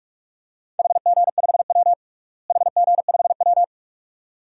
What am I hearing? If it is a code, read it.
Morse code, heard as a keyed tone, about 46 wpm, 700 Hz. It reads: HG5W HG5W
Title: DXCC-100 Call Signs - 4 Characters - Repeated 2x - 45wpm